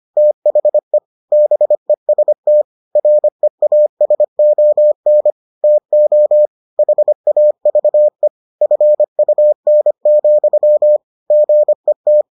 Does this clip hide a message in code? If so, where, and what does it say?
Yes: Morse code from 0.2 to 12.2 s, heard as a keyed tone, about 25 wpm, 600 Hz. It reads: THE BEST REASON TO HAVE FUN, GET